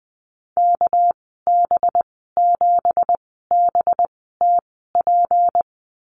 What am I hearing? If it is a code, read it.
Morse code, heard as a keyed tone, about 20 wpm, spaced as for 15 wpm, 700 Hz. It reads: KB7BTP
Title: 219 Lesson 35g - Number 0 - Call Signs 20wpm